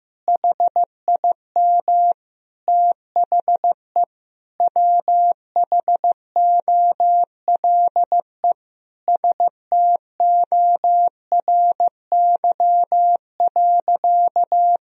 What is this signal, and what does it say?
Morse code, heard as a keyed tone, about 15 wpm, 700 Hz. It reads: HIM THE WHOLE STORY.
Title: Sentences from Top 1000 Words - Review Part 1 - 15wpm